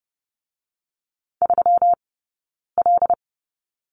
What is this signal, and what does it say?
Morse code, heard as a keyed tone, about 30 wpm, 700 Hz. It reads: 3 L
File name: Lesson 20d - Number 3 - Two Characters 30wpm